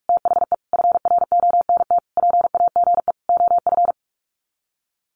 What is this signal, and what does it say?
Morse code, heard as a keyed tone, about 44 wpm, 700 Hz. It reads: THE FRONT PAGE OF
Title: Sets of 5 Words Encore 45wpm